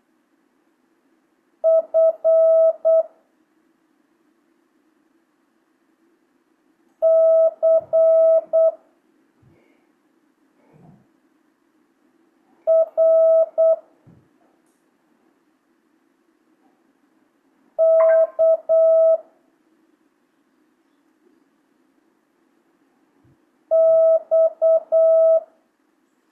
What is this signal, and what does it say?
Morse code, heard as a keyed tone, about 8 wpm, 650 Hz. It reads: F C R K X
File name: jh8cbh / 2025cw